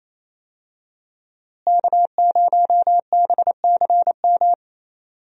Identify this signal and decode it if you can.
Morse code, heard as a keyed tone, about 28 wpm, 700 Hz. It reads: K0BCM
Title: USA Call Signs - 5 Characters - Repeated 2x - 28wpm